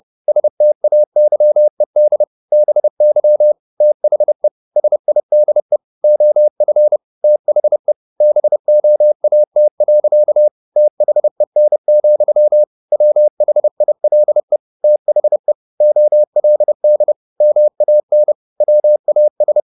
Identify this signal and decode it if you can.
Morse code, heard as a keyed tone, about 30 wpm, 600 Hz. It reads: STAYED BY THE SIDE OF THE BOAT. THEN, WHILE THE OLD MAN WAS
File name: the old man and the sea / the old man and the sea 1 30wpm